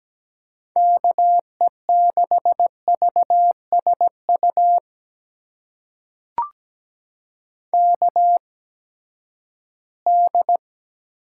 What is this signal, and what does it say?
Morse code, heard as a keyed tone, about 17 wpm, spaced as for 12 wpm, 700 Hz. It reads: KE6VSU KD
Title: USA Call Signs - 6 Characters - 8x Character Spacing - 17wpm